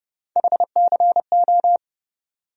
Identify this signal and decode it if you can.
Morse code, heard as a keyed tone, about 30 wpm, 700 Hz. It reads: HCO
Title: USA Call Sign Postfixes - 3 Letters 30wpm